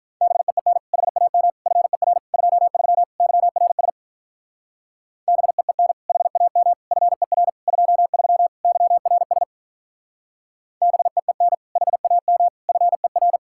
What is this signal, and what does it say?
Morse code, heard as a keyed tone, about 43 wpm, 700 Hz. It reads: BEEN HAM FER 23 YRS BEEN HAM FER 23 YRS BEEN HAM FER
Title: Speed-Racing Common QSO Phrases - 35wpm